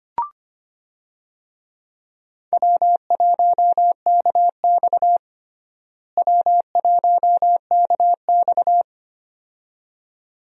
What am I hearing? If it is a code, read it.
Morse code, heard as a keyed tone, about 25 wpm, 700 Hz. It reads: W1KX W1KX
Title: USA Call Signs - 4 Characters - Repeated 2x - 25wpm